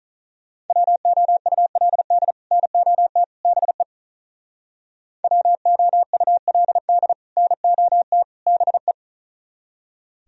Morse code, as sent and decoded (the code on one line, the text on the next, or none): .-- --- ..- .-.. -.. / -. --- - / -... / .-- --- ..- .-.. -.. / -. --- - / -... .
WOULD NOT BE WOULD NOT BE